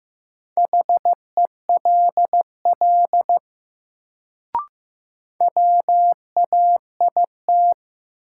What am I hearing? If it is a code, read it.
Morse code, heard as a keyed tone, about 15 wpm, 700 Hz. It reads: HELL WAIT
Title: Sets of 5 Words Encore - New Word Drill - 15wpm